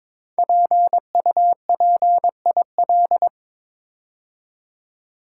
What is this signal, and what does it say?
Morse code, heard as a keyed tone, about 22 wpm, 700 Hz. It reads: PUPIL
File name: 5 Letter Words - First 3 Letters Given - 22wpm